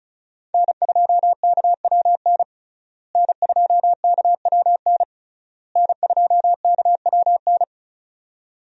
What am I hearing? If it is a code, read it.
Morse code, heard as a keyed tone, about 34 wpm, 700 Hz. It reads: N2KWN N2KWN N2KWN
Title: USA Call Signs - 5 Characters - Repeated 3x - 35wpm